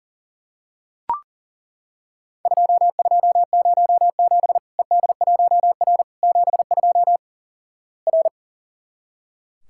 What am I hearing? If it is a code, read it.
Morse code, heard as a keyed tone, about 37 wpm, 700 Hz. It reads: 2207 ED1R 72 R
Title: Worked All Europe DX Contest - QTC Exchange - 40wpm